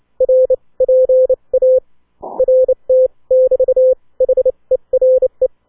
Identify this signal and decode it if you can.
Morse code, heard as a keyed tone, about 29 wpm, spaced as for 21 wpm, 520 Hz. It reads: R PART <BT> HERE